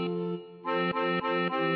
There is no code.